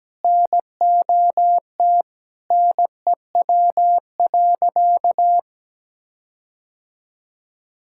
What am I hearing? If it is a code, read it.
Morse code, heard as a keyed tone, about 17 wpm, 700 Hz. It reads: NOT NEW.
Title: Sentences from Top 1600 Words 17wpm